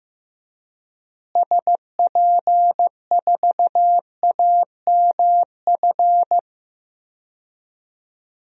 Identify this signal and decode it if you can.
Morse code, heard as a keyed tone, about 15 wpm, 700 Hz. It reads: SP4AMF
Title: DXCC-100 Call Signs - 6 Characters - Repeated 2x - 15wpm